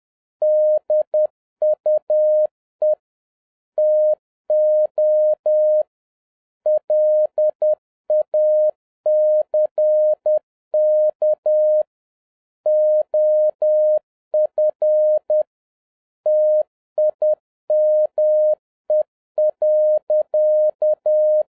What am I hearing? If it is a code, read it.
Morse code, heard as a keyed tone, about 10 wpm, 610 Hz. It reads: DUE TO LACK OF TIME.
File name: 2011-07-mp3 / n56 BACK FROM A BIG ACCIDENT 10wpm